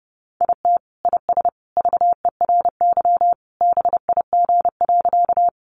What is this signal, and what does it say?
Morse code, heard as a keyed tone, about 30 wpm, 700 Hz. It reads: IT IS VERY BIG.
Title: Easy Phrases 30wpm